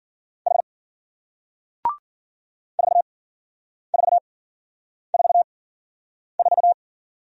Morse code, heard as a keyed tone, about 52 wpm, spaced as for 9 wpm, 700 Hz. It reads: H 4444